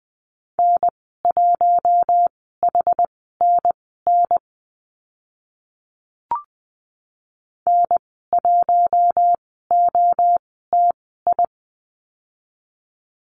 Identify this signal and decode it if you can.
Morse code, heard as a keyed tone, about 20 wpm, spaced as for 15 wpm, 700 Hz. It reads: N1HNN N1OTI
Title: 062 Lesson 11g - Letter h - Call Signs 20wpm